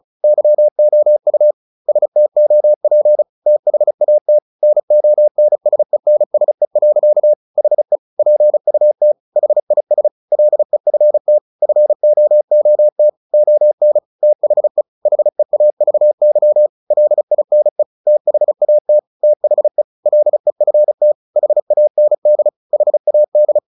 Morse code, heard as a keyed tone, about 34 wpm, 600 Hz. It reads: YOU STOP THAT NONSENSE. HE PUT HIS LEFT FOOT ON THE HEAVY LINE THAT THE LEFT HAND HAD
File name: the old man and the sea / the old man and the sea 1 35wpm